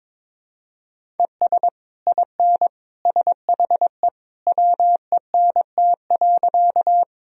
Morse code, heard as a keyed tone, about 21 wpm, 700 Hz. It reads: ES IN SHE WENT.
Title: Sentences from Top 200 Words and Abbreviations 22wpm